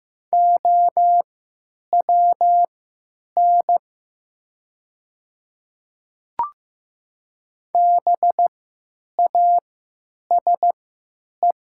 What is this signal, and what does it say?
Morse code, heard as a keyed tone, about 15 wpm, spaced as for 9 wpm, 700 Hz. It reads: OWN BASE
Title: 3x Character Spacing - Top 300 Words - 15wpm